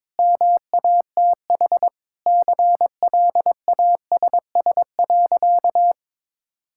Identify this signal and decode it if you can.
Morse code, heard as a keyed tone, about 22 wpm, 700 Hz. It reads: MATH CLASS.
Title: Sentences from Top 1000 Words - Review Part 6 - 22wpm